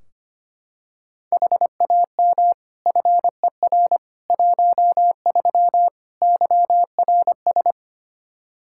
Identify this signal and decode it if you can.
Morse code, heard as a keyed tone, about 25 wpm, 700 Hz. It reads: HAM FER 13 YRS
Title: K1USN SST QSO Practice 25wpm